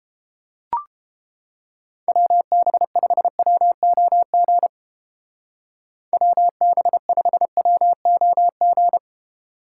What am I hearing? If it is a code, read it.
Morse code, heard as a keyed tone, about 29 wpm, 700 Hz. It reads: WB5WOG WB5WOG